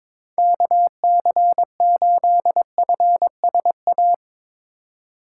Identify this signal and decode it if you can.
Morse code, heard as a keyed tone, about 22 wpm, 700 Hz. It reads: KC8FSA
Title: USA Call Signs - 3x Character Spacing - 22wpm